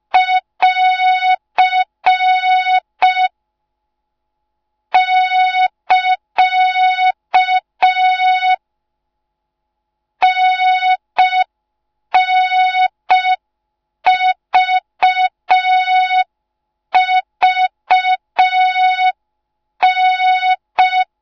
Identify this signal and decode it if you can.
Morse code, heard as a keyed tone, about 5 wpm, 3700 Hz. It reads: <AR> <KA> NNVVN